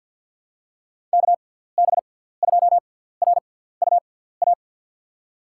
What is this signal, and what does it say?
Morse code, heard as a keyed tone, about 53 wpm, spaced as for 17 wpm, 700 Hz. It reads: KB2RUA
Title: USA Call Signs - 6 Characters - 6x Character Spacing - 50wpm